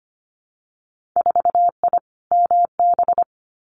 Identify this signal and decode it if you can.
Morse code, heard as a keyed tone, about 25 wpm, 690 Hz. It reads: 4I MB
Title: Winter Field Day Exchanges Encore 25wpm